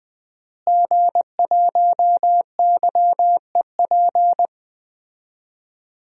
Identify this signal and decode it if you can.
Morse code, heard as a keyed tone, about 20 wpm, 700 Hz. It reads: G1YEP
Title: DXCC-100 Call Signs - 5 Characters - Repeated 2x - 20wpm